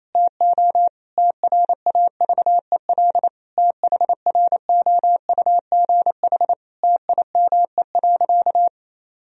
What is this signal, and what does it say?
Morse code, heard as a keyed tone, about 28 wpm, 700 Hz. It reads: TO TRAVEL THROUGH TIME.
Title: Sentences from Top 1000 Words - Review Part 2 - 28wpm